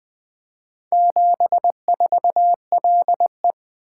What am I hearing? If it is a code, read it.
Morse code, heard as a keyed tone, about 20 wpm, 700 Hz. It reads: 74LE
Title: Four Letter-Number Combinations - 20wpm